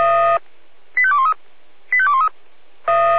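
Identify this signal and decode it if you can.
Five-tone selective call (ZVEI), heard as quick runs of five tones, tones of about 70 ms. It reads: 85213 85213
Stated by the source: ZVEI five-tone call 85211 received by SDR.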